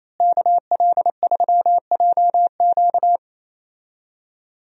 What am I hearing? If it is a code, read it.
Morse code, heard as a keyed tone, about 28 wpm, 700 Hz. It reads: KL3JQ
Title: USA Call Signs - 5 Characters - Repeated 2x - 28wpm